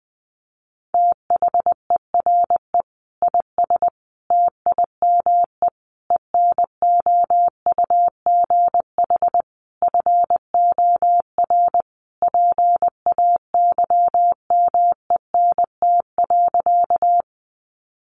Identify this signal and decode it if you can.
Morse code, heard as a keyed tone, about 20 wpm, 700 Hz. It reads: THERE IS TIME ENOUGH FOR PAYMENT.
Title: Sentences from Top 1200 Words 20wpm